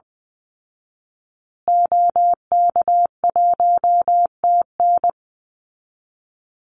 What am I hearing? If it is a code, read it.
Morse code, heard as a keyed tone, about 20 wpm, 700 Hz. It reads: OK1TN
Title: Call Signs Repeated 3x - 20wpm